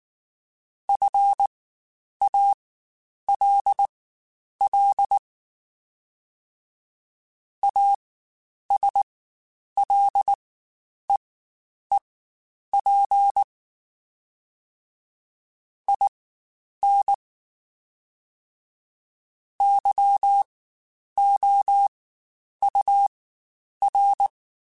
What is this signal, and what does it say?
Morse code, heard as a keyed tone, about 19 wpm, spaced as for 9 wpm, 800 Hz. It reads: FALL ASLEEP IN YOUR
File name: Quotes Spaced / Quotes Spaced-mp3s / ChildAsleep sp @18